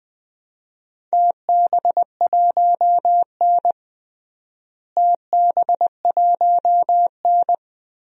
Morse code, heard as a keyed tone, about 20 wpm, 700 Hz. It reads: TB1N TB1N
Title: DXCC-100 Call Signs - 4 Characters - Repeated 3x - 20wpm